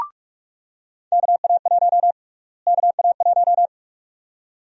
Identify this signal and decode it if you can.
Morse code, heard as a keyed tone, about 41 wpm, 700 Hz. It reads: KA1 KA1
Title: USA Call Sign Prefixes - Repeated 2x - 45wpm